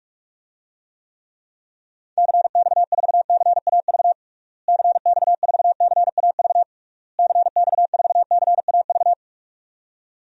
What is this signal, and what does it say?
Morse code, heard as a keyed tone, about 46 wpm, 700 Hz. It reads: CX4CAV CX4CAV CX4CAV
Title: DXCC-100 Call Signs - 6 Characters - Repeated 3x - 45wpm